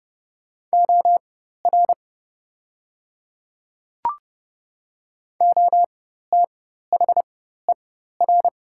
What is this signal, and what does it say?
Morse code, heard as a keyed tone, about 30 wpm, spaced as for 14 wpm, 700 Hz. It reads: OR OTHER
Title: 4x Character Spacing - Top 100 Words - 30wpm